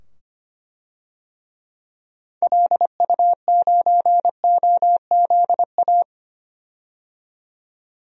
Morse code, heard as a keyed tone, about 25 wpm, 700 Hz. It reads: LU9OZA
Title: DXCC-100 Call Signs - 6 Characters - 25wpm